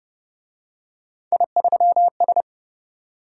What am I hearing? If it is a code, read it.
Morse code, heard as a keyed tone, about 28 wpm, 700 Hz. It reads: I3S